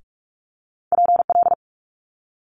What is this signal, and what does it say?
Morse code, heard as a keyed tone, about 46 wpm, 700 Hz. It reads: PL
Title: Multi-letter Phonemes Quick Fire 45wpm